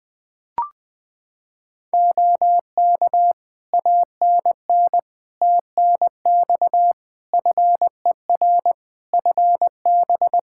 Morse code, heard as a keyed tone, about 20 wpm, 700 Hz. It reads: OK ANN TNX FER FB
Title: Practice QSO - Exchange 4 - Ending - 20wpm